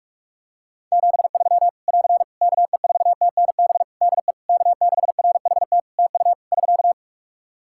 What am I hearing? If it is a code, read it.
Morse code, heard as a keyed tone, about 46 wpm, 700 Hz. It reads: 73 <AR> KE4TNB DE K6RHT TU <SK>
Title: Practice QSO - Exchange 4 - Ending - 45wpm